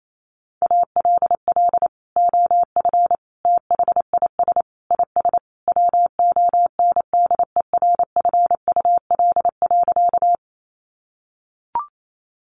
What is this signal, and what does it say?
Morse code, heard as a keyed tone, about 28 wpm, 700 Hz. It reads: ALL OF THIS IS WONDERFUL.
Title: Sentences from Top 1100 Words 28wpm